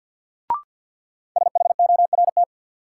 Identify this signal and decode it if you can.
Morse code, heard as a keyed tone, about 52 wpm, 700 Hz. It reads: SHORT
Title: Sentences from Top 400 Words - New Word Drill - 50wpm